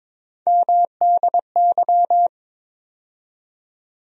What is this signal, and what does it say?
Morse code, heard as a keyed tone, about 22 wpm, 700 Hz. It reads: MDY
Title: USA Call Sign Postfixes - 3 Letters 22wpm